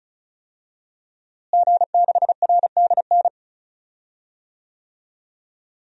Morse code, heard as a keyed tone, about 36 wpm, 700 Hz. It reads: G6RDN
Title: DXCC-100 Call Signs - 5 Characters - Repeated 2x - 35wpm